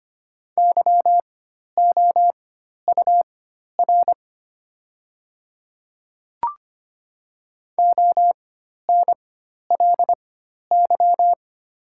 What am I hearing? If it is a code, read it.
Morse code, heard as a keyed tone, about 25 wpm, spaced as for 12 wpm, 690 Hz. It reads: YOUR ONLY
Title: 4x Character Spacing - Top 200 Words - 25wpm